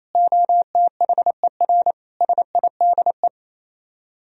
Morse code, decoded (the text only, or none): OTHER SIDE